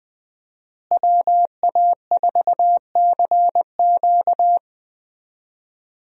Morse code, heard as a keyed tone, about 21 wpm, 700 Hz. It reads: WA4CQ
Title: USA Call Signs - 5 Characters - Repeated 2x - 20wpm